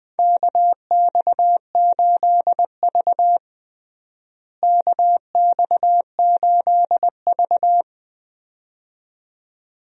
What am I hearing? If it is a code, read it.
Morse code, heard as a keyed tone, about 20 wpm, 700 Hz. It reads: KX8V KX8V
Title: USA Call Signs - 4 Characters - Repeated 2x - 20wpm